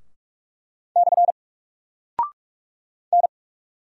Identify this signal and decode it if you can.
Morse code, heard as a keyed tone, about 44 wpm, 700 Hz. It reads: / N
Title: Single Character (Letter:Number:Punctuation:Prosign) 45wpm